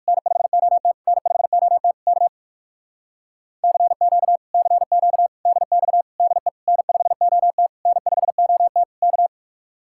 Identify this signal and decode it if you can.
Morse code, heard as a keyed tone, about 56 wpm, 700 Hz. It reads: N5OT N5OT K CQ CQ DX DE N5OT N5OT K
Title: Speed-Racing Common QSO Phrases - 45wpm